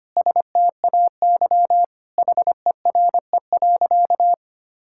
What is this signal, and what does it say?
Morse code, heard as a keyed tone, about 25 wpm, 700 Hz. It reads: STAY HERE.
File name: practice / 25wpm / Sentences with 4-Word Phrases - 25wpm